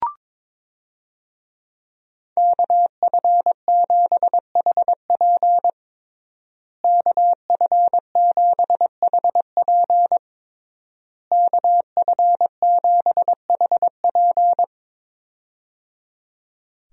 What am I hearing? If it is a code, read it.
Morse code, heard as a keyed tone, about 22 wpm, 700 Hz. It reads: KF7HP KF7HP KF7HP
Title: USA Call Signs - 5 Characters - Repeated 3x - 22wpm